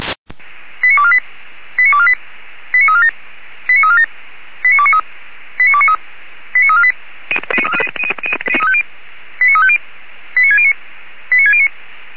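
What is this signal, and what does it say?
Five-tone selective call (ZVEI), heard as quick runs of five tones, tones of about 70 ms. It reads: 89248 89248 89347 89347 89293 89293 89348 89350 89609 89609
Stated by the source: ZVEI five-tone call 89549 received by SDR, channel 464, recorded by Feuerwehr Orlamünde.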